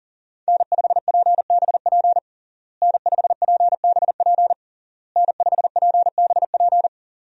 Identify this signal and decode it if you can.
Morse code, heard as a keyed tone, about 40 wpm, 700 Hz. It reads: N5PBP N5PBP N5PBP
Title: USA Call Signs - 5 Characters - Repeated 3x - 40wpm